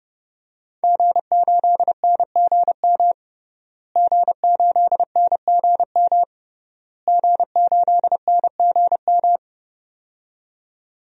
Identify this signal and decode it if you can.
Morse code, heard as a keyed tone, about 30 wpm, 700 Hz. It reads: G8NGM G8NGM G8NGM
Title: DXCC-100 Call Signs - 5 Characters - Repeated 3x - 30wpm